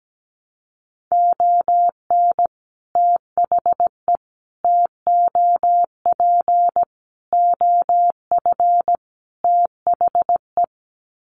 Morse code, heard as a keyed tone, about 17 wpm, 700 Hz. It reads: ON THE TOP OF THE
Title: Sets of 5 Words Encore 17wpm